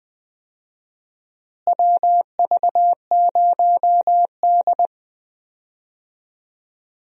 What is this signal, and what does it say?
Morse code, heard as a keyed tone, about 20 wpm, 700 Hz. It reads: WV0D
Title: USA Call Signs - 4 Characters - Repeated 2x - 20wpm